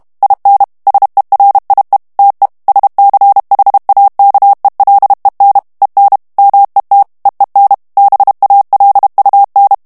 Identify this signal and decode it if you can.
Morse code, heard as a keyed tone, about 32 wpm, 800 Hz. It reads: IN SERIE TE SCHAKELEN EN MET EEN BALUN
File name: HOLLANDAIS texte clair 1 vitesse 30 mots minute